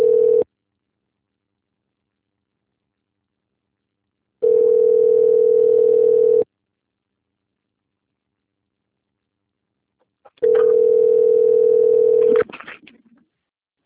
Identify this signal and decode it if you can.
Ringback tone (North America), 440 + 480 Hz.